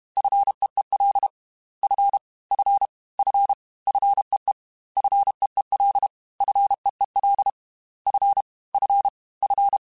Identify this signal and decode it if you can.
Morse code, heard as a keyed tone, about 32 wpm, 800 Hz. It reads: FEEL F F F FEE FEEL FEEL F F F
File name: Fs / Single F mp3s / Single F Prefixes 3x2 Twice @30